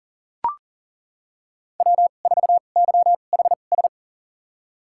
Morse code, heard as a keyed tone, about 38 wpm, spaced as for 28 wpm, 700 Hz. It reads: W4YHS
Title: USA Call Signs - 2x Character Spacing - 40wpm